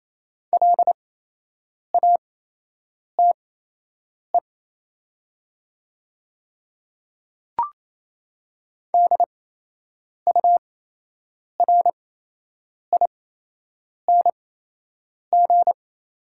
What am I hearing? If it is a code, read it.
Morse code, heard as a keyed tone, about 28 wpm, spaced as for 8 wpm, 700 Hz. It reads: LATE DURING
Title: 8x Character Spacing - Top 500 Words - 28wpm